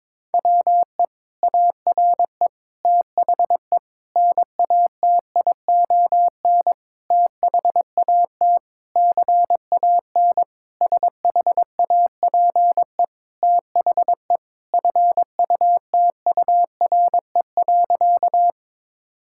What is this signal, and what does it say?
Morse code, heard as a keyed tone, about 21 wpm, 700 Hz. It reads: WE ARE THE NATION THAT CAN SHAPE THE FUTURE.
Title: Sentences from Top 1000 Words - Review Part 6 - 22wpm